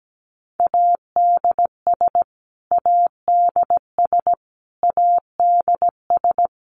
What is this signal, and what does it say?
Morse code, heard as a keyed tone, about 17 wpm, 700 Hz. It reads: ADS ADS ADS
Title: Abbreviations Repeated 3x - 17wpm